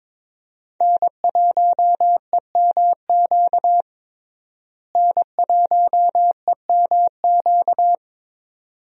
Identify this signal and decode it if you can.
Morse code, heard as a keyed tone, about 22 wpm, 700 Hz. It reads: N1EMQ N1EMQ
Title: USA Call Signs - 5 Characters - Repeated 2x - 22wpm